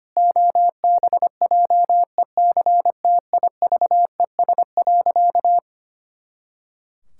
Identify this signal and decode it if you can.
Morse code, heard as a keyed tone, about 25 wpm, 700 Hz. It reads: OBJECTIVES.
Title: Pangram Challenge v2 - 25wpm